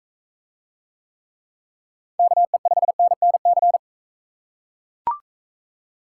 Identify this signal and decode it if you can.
Morse code, heard as a keyed tone, about 41 wpm, 700 Hz. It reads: KE5NNC